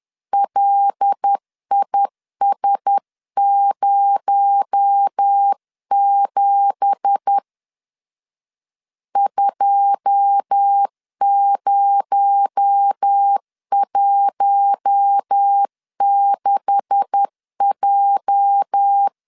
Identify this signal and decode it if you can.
Morse code, heard as a keyed tone, about 10 wpm, 800 Hz. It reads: LIS07 2016J
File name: CW / DET / 1 9 1500C S A 10wpm 4511